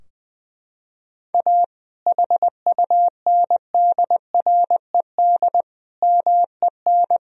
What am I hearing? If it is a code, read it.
Morse code, heard as a keyed tone, about 20 wpm, 700 Hz. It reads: A HUNDRED MEN
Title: Sentences from Top 1000 Words - Review Part 2 - 20wpm